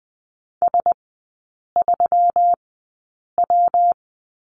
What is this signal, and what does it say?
Morse code, heard as a keyed tone, about 20 wpm, spaced as for 8 wpm, 700 Hz. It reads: S3W
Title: 212 Lesson 34e - Number 8 - Three Characters 20wpm